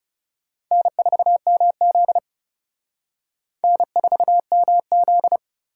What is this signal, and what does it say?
Morse code, heard as a keyed tone, about 40 wpm, 700 Hz. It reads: N4MZ N4MZ